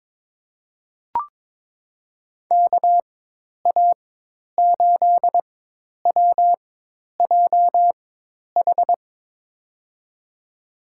Morse code, heard as a keyed tone, about 22 wpm, spaced as for 10 wpm, 700 Hz. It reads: KA8WJH
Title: USA Call Signs - 4x Character Spacing - 22wpm